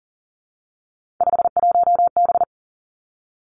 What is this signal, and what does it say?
Morse code, heard as a keyed tone, about 41 wpm, 700 Hz. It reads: 51B
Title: Three Challenging Letter-Number Combinations 40wpm